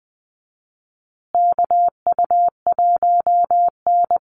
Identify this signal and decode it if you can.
Morse code, heard as a keyed tone, about 20 wpm, 700 Hz. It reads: KU1N
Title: Call Signs Repeated 2x - 20wpm